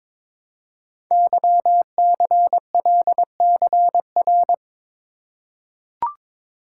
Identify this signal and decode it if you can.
Morse code, heard as a keyed tone, about 23 wpm, 700 Hz. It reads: YCLCR
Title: Code Groups - 5 Letters - 22wpm